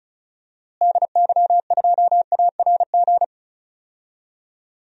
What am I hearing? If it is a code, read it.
Morse code, heard as a keyed tone, about 36 wpm, 700 Hz. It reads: DY2ARG
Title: DXCC-100 Call Signs - 6 Characters - Repeated 3x - 35wpm